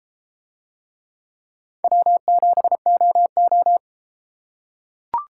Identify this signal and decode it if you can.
Morse code, heard as a keyed tone, about 33 wpm, 700 Hz. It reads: W7OO